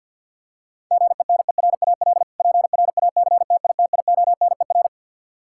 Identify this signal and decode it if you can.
Morse code, heard as a keyed tone, about 46 wpm, 700 Hz. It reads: GENERAL PRACTITIONER
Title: K1USN SST QSO Practice 50wpm